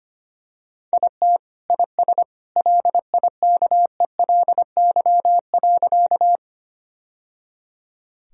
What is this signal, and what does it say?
Morse code, heard as a keyed tone, about 25 wpm, 700 Hz. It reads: IT IS LIKELY.
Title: Sentences from Top 1000 Words - Review Part 4 - 25wpm